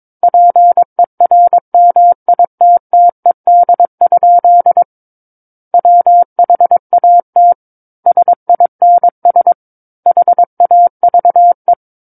Morse code, heard as a keyed tone, about 22 wpm, 700 Hz. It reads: PERMITTED? WHAT SINS HAVE
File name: Book One / 13-Chapter 13 - 22 WPM